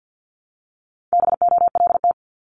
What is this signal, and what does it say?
Morse code, heard as a keyed tone, about 50 wpm, 700 Hz. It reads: BOLT
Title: Multi-letter Phonemes Repeated 3x 50wpm